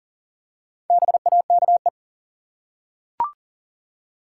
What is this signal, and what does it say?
Morse code, heard as a keyed tone, about 40 wpm, 700 Hz. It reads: BAKE
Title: Four Letter Words - 40wpm